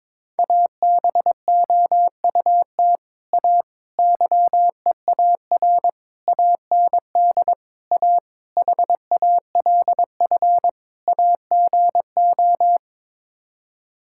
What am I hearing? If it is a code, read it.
Morse code, heard as a keyed tone, about 22 wpm, 700 Hz. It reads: ABOUT A YEAR AND A HALF AGO
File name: practice / 22wpm / Sets of 7 Words - 22wpm